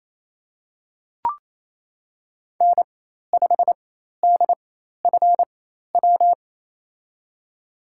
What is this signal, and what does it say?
Morse code, heard as a keyed tone, about 28 wpm, spaced as for 13 wpm, 700 Hz. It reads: N5DFW
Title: USA Call Signs - 4x Character Spacing - 28wpm